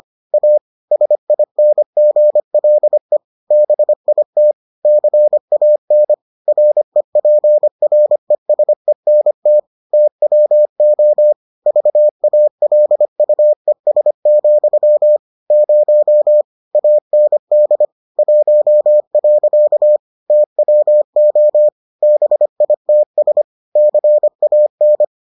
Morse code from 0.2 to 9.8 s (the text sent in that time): A SINGLE BIT CAN REPRESENT